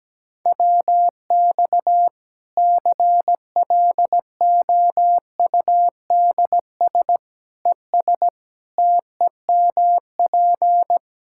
Morse code, heard as a keyed tone, about 17 wpm, 700 Hz. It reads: WX CLOUDS ES TEMP
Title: Practice QSO - Exchange 2 - Info - 17wpm